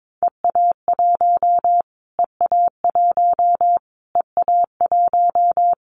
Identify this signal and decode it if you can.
Morse code, heard as a keyed tone, about 22 wpm, 700 Hz. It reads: EA1 EA1 EA1
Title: Callsign Prefixes for Top 25 DXCC - Repeated 3x - 22wpm